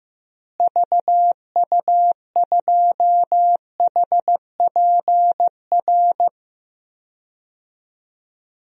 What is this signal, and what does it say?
Morse code, heard as a keyed tone, about 15 wpm, 700 Hz. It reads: VU2HPR
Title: DXCC-100 Call Signs - 6 Characters - Repeated 2x - 15wpm